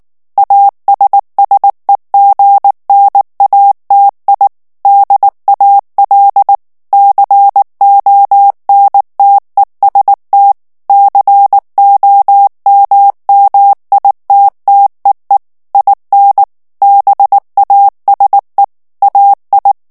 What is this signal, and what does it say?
Morse code, heard as a keyed tone, about 19 wpm, 800 Hz. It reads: ASSEGNATI DAL CONTEST COMMITTEE IN BASE AI
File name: morse mp3 / ITALIEN texte clair 2 vitesse 18 mots minute